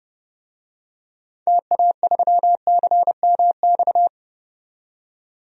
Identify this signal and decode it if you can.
Morse code, heard as a keyed tone, about 30 wpm, 700 Hz. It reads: TA3CMX
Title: DXCC-100 Call Signs - 6 Characters - 30wpm